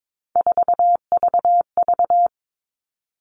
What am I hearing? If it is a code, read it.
Morse code, heard as a keyed tone, about 22 wpm, 700 Hz. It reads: 4VV
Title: All Cut Numbers - 3-Digit 22wpm